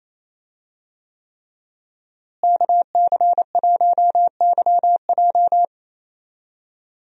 Morse code, heard as a keyed tone, about 28 wpm, 700 Hz. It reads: KC1YJ
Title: USA Call Signs - 5 Characters - Repeated 3x - 28wpm